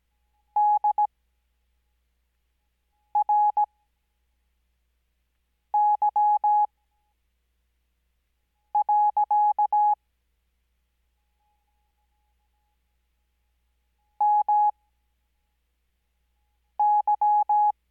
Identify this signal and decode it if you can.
Morse code, heard as a keyed tone, about 17 wpm, spaced as for 4 wpm, 840 Hz. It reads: DRY. MY
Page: Farnsworth Spacing: 15WPM Character speed Spaced at 5WPM.